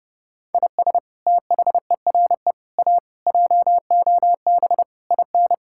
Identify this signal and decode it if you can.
Morse code, heard as a keyed tone, about 30 wpm, 700 Hz. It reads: IS THERE A JOB IN
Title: Sentences from Top 1600 Words 30wpm